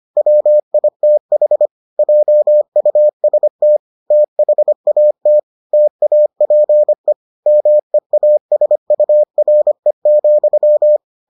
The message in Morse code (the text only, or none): WITH JUST THAT TAPE MEASURE,